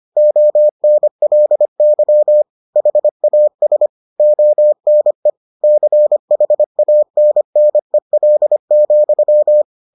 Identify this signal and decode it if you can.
Morse code, heard as a keyed tone, about 25 wpm, 600 Hz. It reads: ONLY HAS ONE CHANNEL,